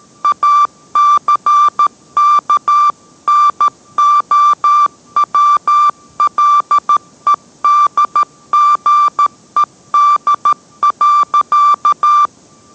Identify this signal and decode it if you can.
Morse code, heard as a keyed tone, about 13 wpm, 1180 Hz. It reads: ACKNOWLEDGED.